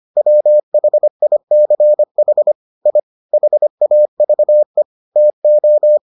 Morse code, heard as a keyed tone, about 25 wpm, 600 Hz. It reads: WHICH I HAVE TO